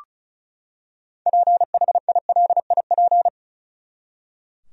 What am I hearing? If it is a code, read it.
Morse code, heard as a keyed tone, about 34 wpm, 700 Hz. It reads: PHILIP
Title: USA Common Names - 35wpm